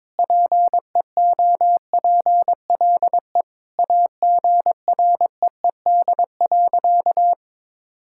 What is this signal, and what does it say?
Morse code, heard as a keyed tone, about 22 wpm, 700 Hz. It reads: PEOPLE AGREED.
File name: practice / 22wpm / Sentences with 4-Word Phrases - 22wpm